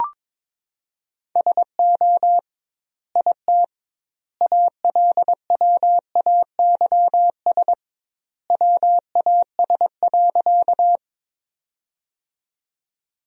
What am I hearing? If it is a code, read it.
Morse code, heard as a keyed tone, about 22 wpm, 700 Hz. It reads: SO IT ALWAYS WAS.
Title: 2x Word-Spacing - Sentences from Top 300 Words - 22wpm